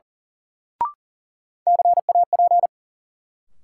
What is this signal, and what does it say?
Morse code, heard as a keyed tone, about 39 wpm, 700 Hz. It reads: CAP
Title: Sentences from Top 1600 Words - New Word Drill - 40wpm